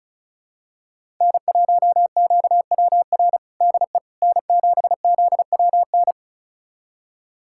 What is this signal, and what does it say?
Morse code, heard as a keyed tone, about 36 wpm, 700 Hz. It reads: N1QWR DE N7ZWN
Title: USA Callsign DE Callsign - 5 Characters - 35wpm